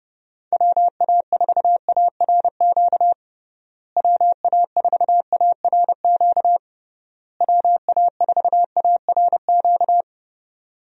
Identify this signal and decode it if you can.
Morse code, heard as a keyed tone, about 31 wpm, 700 Hz. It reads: WA4ARQ WA4ARQ WA4ARQ
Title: USA Call Signs - 6 Characters - Repeated 3x - 30wpm